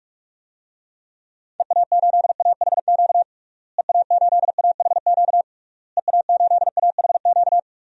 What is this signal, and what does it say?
Morse code, heard as a keyed tone, about 44 wpm, 700 Hz. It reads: EA8AHQ EA8AHQ EA8AHQ
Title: DXCC-100 Call Signs - 6 Characters - Repeated 3x - 45wpm